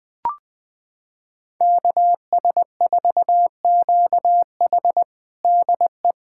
Morse code, heard as a keyed tone, about 20 wpm, 700 Hz. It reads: KS4QH DE
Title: USA Callsign DE Callsign - 5 Characters - 20wpm